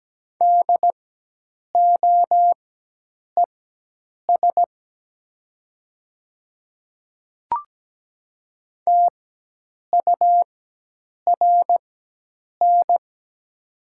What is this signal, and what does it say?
Morse code, heard as a keyed tone, about 17 wpm, spaced as for 8 wpm, 700 Hz. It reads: DOES TURN
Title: 4x Character Spacing - Top 200 Words - 17wpm